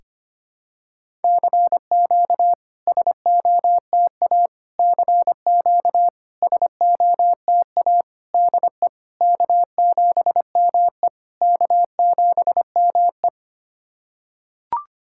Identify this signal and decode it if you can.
Morse code, heard as a keyed tone, about 25 wpm, 700 Hz. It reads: CQ SOTA CQ SOTA DE K7ME K7ME
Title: SOTA QSO Practice 25wpm